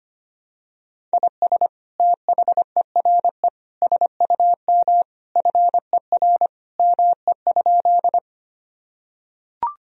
Morse code, heard as a keyed tone, about 25 wpm, 700 Hz. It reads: IS THERE SUM FER ME?
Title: Sentences from Top 200 Words and Abbreviations 25wpm